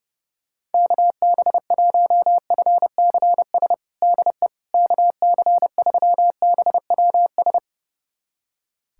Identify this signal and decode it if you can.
Morse code, heard as a keyed tone, about 30 wpm, 700 Hz. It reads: KB1FCS DE KC3BWS
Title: USA Callsign DE Callsign 30wpm